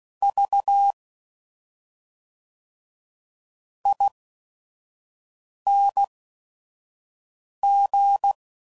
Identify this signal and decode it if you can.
Morse code, heard as a keyed tone, about 16 wpm, spaced as for 5 wpm, 800 Hz. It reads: V ING
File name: Suffixes / Special ING's / SlowToFaster ING's